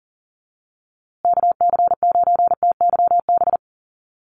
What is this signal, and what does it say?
Morse code, heard as a keyed tone, about 39 wpm, 700 Hz. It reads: KC9TYB
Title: USA Call Signs Repeated 3x 40wpm